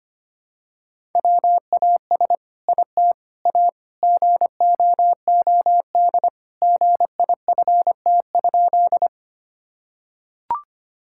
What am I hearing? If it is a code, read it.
Morse code, heard as a keyed tone, about 25 wpm, 700 Hz. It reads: WAS IT A GOOD GIFT?
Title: Sentences from Top 1000 Words - Review Part 6 - 25wpm